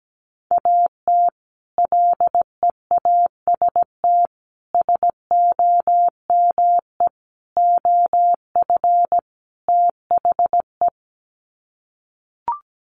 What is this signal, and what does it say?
Morse code, heard as a keyed tone, about 17 wpm, 700 Hz. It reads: AT LEAST SOME OF THE
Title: Sets of 5 Words Encore 17wpm